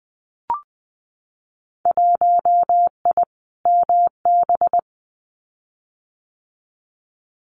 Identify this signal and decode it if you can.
Morse code, heard as a keyed tone, about 20 wpm, 700 Hz. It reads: 1I MB
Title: Winter Field Day Exchanges Encore 20wpm